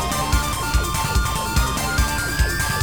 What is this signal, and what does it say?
There is no code.